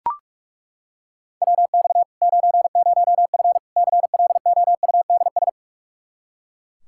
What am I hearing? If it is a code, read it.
Morse code, heard as a keyed tone, about 42 wpm, 700 Hz. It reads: WX 90F CLOUDS
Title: Practice QSO - Exchange 2 - Info - 45wpm